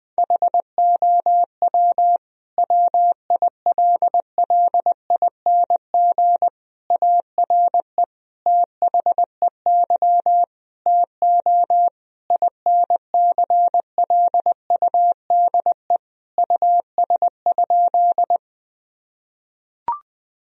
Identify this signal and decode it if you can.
Morse code, heard as a keyed tone, about 19 wpm, 700 Hz. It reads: HOW WILLING ARE THEY TO INCLUDE US?
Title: Sentences from Top 1000 Words - Review Part 3 - 20wpm